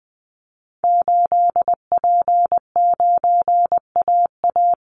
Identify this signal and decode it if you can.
Morse code, heard as a keyed tone, about 20 wpm, 700 Hz. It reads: 8P9AA
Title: Call Signs - 20wpm